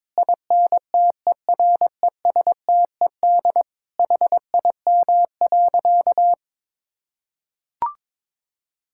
Morse code, heard as a keyed tone, about 22 wpm, 700 Hz. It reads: INTERESTED HIM.
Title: Sentences from Top 1000 Words - Review Part 6 - 22wpm